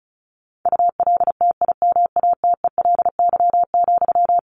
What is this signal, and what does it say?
Morse code, heard as a keyed tone, about 34 wpm, 700 Hz. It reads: ULTIMATELY,
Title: Sentences from Top 1200 Words 35wpm